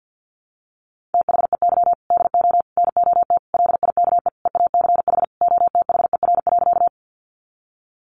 Morse code, heard as a keyed tone, about 50 wpm, 700 Hz. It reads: THEY DO NOT LIKE EACH OTHER.
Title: Easy Phrases 50wpm